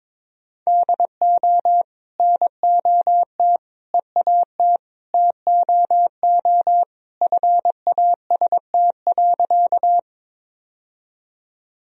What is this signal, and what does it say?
Morse code, heard as a keyed tone, about 22 wpm, 700 Hz. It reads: DO NOT EAT TOO FAST.
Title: Sentences from Top 1000 Words - Review Part 2 - 22wpm